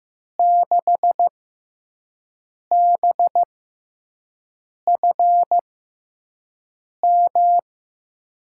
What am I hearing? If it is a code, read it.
Morse code, heard as a keyed tone, about 15 wpm, spaced as for 5 wpm, 700 Hz. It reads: 6BFM
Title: USA Call Signs - 6 Characters - 6x Character Spacing - 15wpm